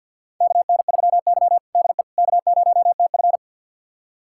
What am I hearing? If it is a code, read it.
Morse code, heard as a keyed tone, about 48 wpm, 700 Hz. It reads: KN3Y DE K0TF
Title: USA Callsign DE Callsign - 4 Characters - 50wpm